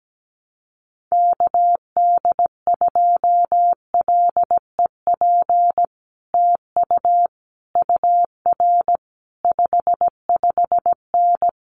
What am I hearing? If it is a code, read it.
Morse code, heard as a keyed tone, about 17 wpm, 700 Hz. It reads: KD2LEP TU UR 55N
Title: Common QSO Phrases Encore 17wpm